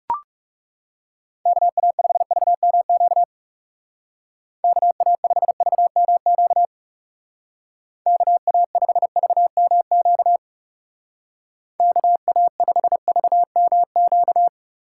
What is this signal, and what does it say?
Morse code, heard as a keyed tone, about 36 wpm, 700 Hz. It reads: KA5VMQ KA5VMQ KA5VMQ KA5VMQ
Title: Speed Racing - USA 2x3 Call Signs 30wpm